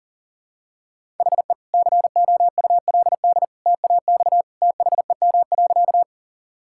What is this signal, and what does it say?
Morse code, heard as a keyed tone, about 40 wpm, 700 Hz. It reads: HE COULD TAX THEM.
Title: Sentences from Top 1000 Words - Review Part 5 - 40wpm